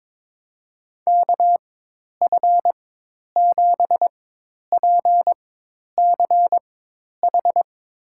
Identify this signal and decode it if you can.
Morse code, heard as a keyed tone, about 22 wpm, spaced as for 10 wpm, 700 Hz. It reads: KF7PCH
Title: USA Call Signs - 6 Characters - 4x Character Spacing - 22wpm